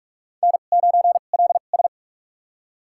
Morse code, heard as a keyed tone, about 46 wpm, spaced as for 33 wpm, 700 Hz. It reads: N9LS
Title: USA Call Signs - 2x Character Spacing - 45wpm